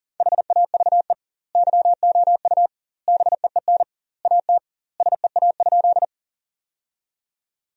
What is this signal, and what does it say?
Morse code, heard as a keyed tone, about 37 wpm, 700 Hz. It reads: HAVE YOU BEEN AT SEA?
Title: 2x Word-Spacing - Sentences from Top 300 Words - 40wpm